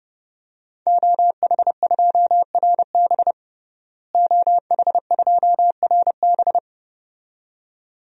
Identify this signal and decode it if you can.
Morse code, heard as a keyed tone, about 32 wpm, 700 Hz. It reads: OH2RB OH2RB
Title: DXCC-100 Call Signs - 5 Characters - Repeated 2x - 30wpm